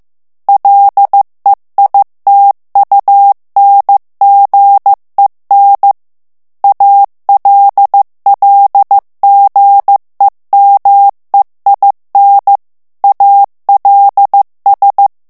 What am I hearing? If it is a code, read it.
Morse code, heard as a keyed tone, about 15 wpm, 800 Hz. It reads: LEITUNGEN ALLGEMEIN ALS
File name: morse mp3 / ALLEMAND texte clair 1 vitesse 14 mots minute